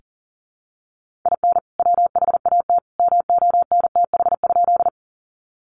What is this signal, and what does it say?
Morse code, heard as a keyed tone, about 40 wpm, 700 Hz. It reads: IN WHAT MONTH?
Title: Sentences from Top 800 Words 40wpm